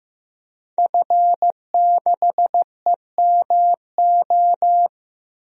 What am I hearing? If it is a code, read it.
Morse code, heard as a keyed tone, about 15 wpm, 700 Hz. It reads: F6EMO
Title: DXCC-100 Call Signs - 5 Characters - Repeated 3x - 15wpm